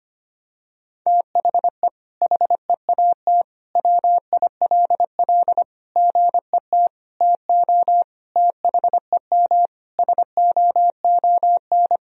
Morse code, heard as a keyed tone, about 25 wpm, 700 Hz. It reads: THE HEAT WILL GET TO THEM SOON
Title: Sentences from Top 1000 Words - Review Part 3 - 25wpm